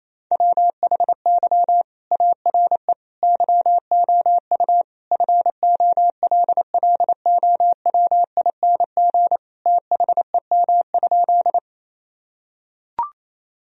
Morse code, heard as a keyed tone, about 28 wpm, 700 Hz. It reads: WHY ARE YOU FOLLOWING THEM?
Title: Sentences from Top 1000 Words - Review Part 4 - 28wpm